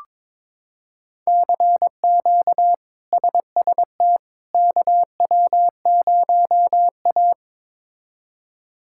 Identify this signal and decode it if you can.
Morse code, heard as a keyed tone, about 22 wpm, 700 Hz. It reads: CQ SST KW0A
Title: K1USN SST QSO Practice - 22wpm